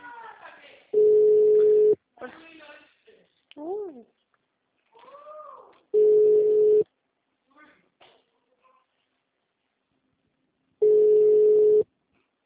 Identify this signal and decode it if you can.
Ringback tone (Europe, 425 Hz), with speech.